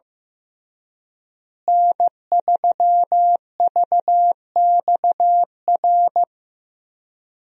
Morse code, heard as a keyed tone, about 15 wpm, 700 Hz. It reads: N3VXR
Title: USA Call Signs - 5 Characters - Repeated 3x - 15wpm